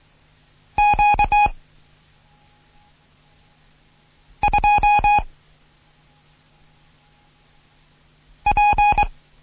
Morse code, heard as a keyed tone, about 23 wpm, 830 Hz. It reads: Q 2 P
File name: Morsen / Kurs A 16